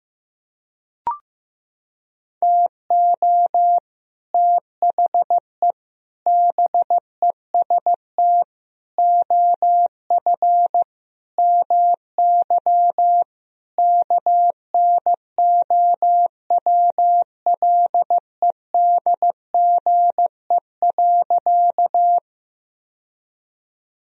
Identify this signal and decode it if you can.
Morse code, heard as a keyed tone, about 15 wpm, 700 Hz. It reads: TO THE BEST OF MY KNOWLEDGE.
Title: Sentences from Top 1000 Words - Review Part 3 - 15wpm